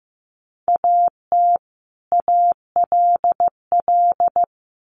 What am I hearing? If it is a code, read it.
Morse code, heard as a keyed tone, about 15 wpm, 700 Hz. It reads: AT ALL
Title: Sets of 2 Words 15wpm